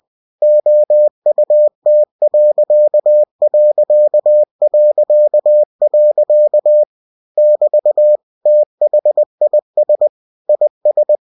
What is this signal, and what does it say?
Morse code, heard as a keyed tone, about 20 wpm, 600 Hz. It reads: OUT.... <BT> THIS IS
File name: cw-books / winnie-the-pooh / winnie-20wpm-page-0094